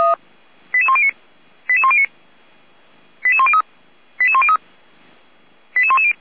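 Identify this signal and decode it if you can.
Five-tone selective call (ZVEI), heard as quick runs of five tones, tones of about 70 ms. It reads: 88109 88109 88103 88103 88110
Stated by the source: ZVEI five-tone selective call 88102 received by SDR, from the fire brigade Feuerwehr Orlamünde audio archive.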